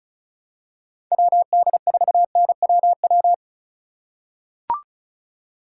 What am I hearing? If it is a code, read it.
Morse code, heard as a keyed tone, about 36 wpm, 700 Hz. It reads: WD4NWW
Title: USA Call Signs - 6 Characters - 4x Character Spacing - 35wpm